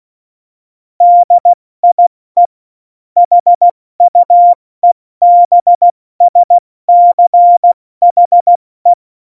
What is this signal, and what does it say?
Morse code, heard as a keyed tone, about 16 wpm, 700 Hz. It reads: DIE HUEBSCHE
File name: Morsekurs / Witze / Witz 11 Juli 17, Tempo 70 F80